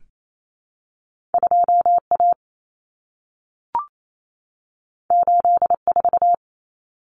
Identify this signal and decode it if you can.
Morse code, heard as a keyed tone, about 28 wpm, 700 Hz. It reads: 2A 84
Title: Common Cut Numbers - 2-Digit 28wpm